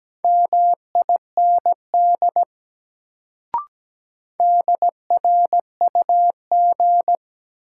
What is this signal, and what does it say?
Morse code, heard as a keyed tone, about 17 wpm, 700 Hz. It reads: MIND DRUG
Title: Sets of 5 Words - New Word Drill - 17wpm